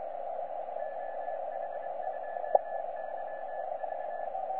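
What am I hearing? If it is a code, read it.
Morse code, heard as a keyed tone, about 23 wpm, 1750 Hz, with radio noise. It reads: OH2YV
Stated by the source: H40T on the 30 m band, CW, Temotu.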